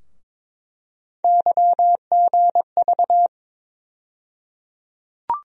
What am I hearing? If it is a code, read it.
Morse code, heard as a keyed tone, about 22 wpm, 700 Hz. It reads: YGV
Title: USA Call Sign Postfixes - 3 Letters 22wpm